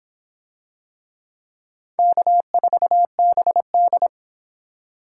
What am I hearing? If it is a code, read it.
Morse code, heard as a keyed tone, about 25 wpm, 700 Hz. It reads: K4BD